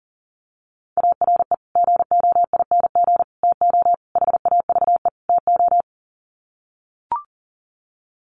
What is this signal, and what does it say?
Morse code, heard as a keyed tone, about 40 wpm, 700 Hz. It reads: ARE GOING TO HAVE TO
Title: Sets of 5 Words 40wpm